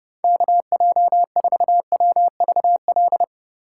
Morse code, heard as a keyed tone, about 30 wpm, 700 Hz. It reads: KJ4WVL
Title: USA Call Signs - 6 Characters - 3x Character Spacing - 30wpm